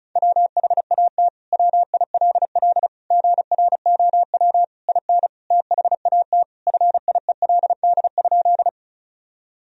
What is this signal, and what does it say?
Morse code, heard as a keyed tone, about 34 wpm, 700 Hz. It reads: WHAT WILL GROW IN THAT FIELD?
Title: Sentences from Top 1000 Words - Review Part 2 - 35wpm